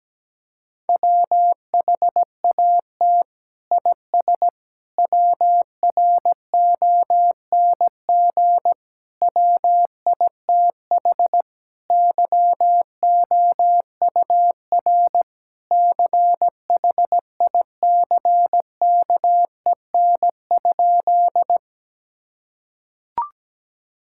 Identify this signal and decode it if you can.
Morse code, heard as a keyed tone, about 17 wpm, 700 Hz. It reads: WHAT IS WRONG WITH YOUR CHICKEN?
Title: Sentences from Top 1000 Words - Review Part 5 - 17wpm